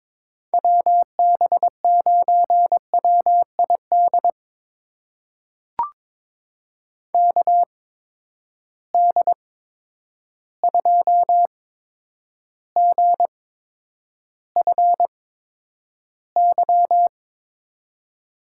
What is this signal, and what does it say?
Morse code, heard as a keyed tone, about 22 wpm, spaced as for 9 wpm, 700 Hz. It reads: WB9WID KD2GFY
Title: USA Call Signs - 6 Characters - 8x Character Spacing - 22wpm